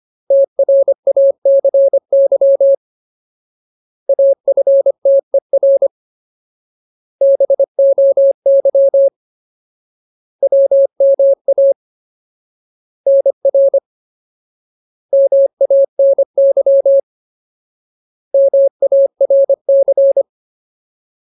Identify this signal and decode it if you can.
Morse code, heard as a keyed tone, about 25 wpm, 550 Hz. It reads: TRACY AFTER BOY WMA NR MANY MARC
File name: Ultimate Word List / 25WPM Ultimate